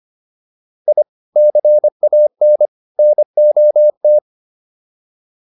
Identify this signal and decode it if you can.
Morse code, heard as a keyed tone, about 25 wpm, 600 Hz. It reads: I CAN NOT